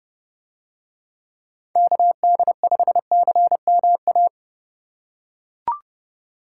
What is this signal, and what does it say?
Morse code, heard as a keyed tone, about 30 wpm, 700 Hz. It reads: KD5CMA